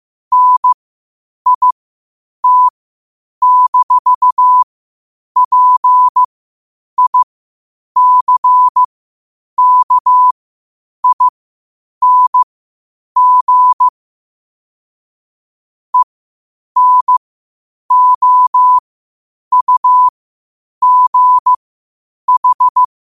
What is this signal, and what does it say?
Morse code, heard as a keyed tone, about 15 wpm, spaced as for 8 wpm, 1000 Hz. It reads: NIT-PICKING ENOUGH
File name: qotdfiles / 8wpm